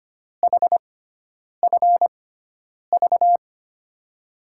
Morse code, heard as a keyed tone, about 25 wpm, spaced as for 9 wpm, 700 Hz. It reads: HFV
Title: USA Call Signs - 6x Character Spacing - 25wpm